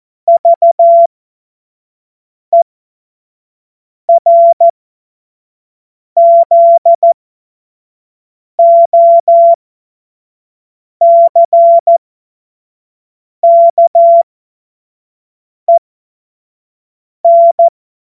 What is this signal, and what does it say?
Morse code, heard as a keyed tone, about 14 wpm, spaced as for 5 wpm, 680 Hz. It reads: VERZOCKEN